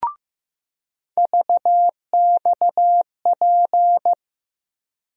Morse code, heard as a keyed tone, about 15 wpm, 700 Hz. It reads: VXP
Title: USA Call Sign Postfixes - 3 Letters 15wpm